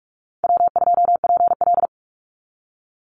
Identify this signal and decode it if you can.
Morse code, heard as a keyed tone, about 44 wpm, 700 Hz. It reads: W2PL
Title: Easy Call Signs Repeated 3x - 45wpm